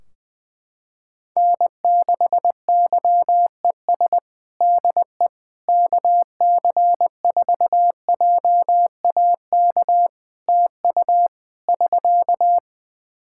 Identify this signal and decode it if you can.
Morse code, heard as a keyed tone, about 20 wpm, 700 Hz. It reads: N6YES DE KC4JAK TU <SK>
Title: K1USN SST QSO Practice 20wpm